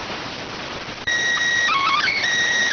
SSTV (slow-scan television), PD 290.